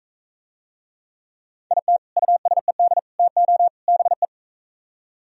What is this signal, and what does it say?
Morse code, heard as a keyed tone, about 44 wpm, 700 Hz. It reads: IT USED TO BE